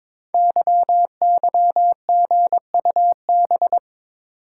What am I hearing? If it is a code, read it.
Morse code, heard as a keyed tone, about 22 wpm, 700 Hz. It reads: YYGUB
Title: Code Groups - 5 Letters - 22wpm